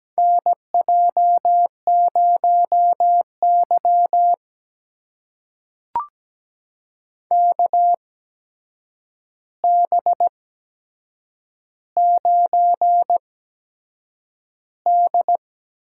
Morse code, heard as a keyed tone, about 17 wpm, spaced as for 7 wpm, 700 Hz. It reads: NJ0Y KB9D
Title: USA Call Signs - 4 Characters - 8x Character Spacing - 17wpm